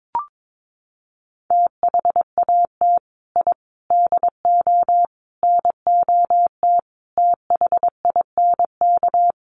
Morse code, heard as a keyed tone, about 22 wpm, 700 Hz. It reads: THAT I DO NOT THINK
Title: Sets of 5 Words Encore 22wpm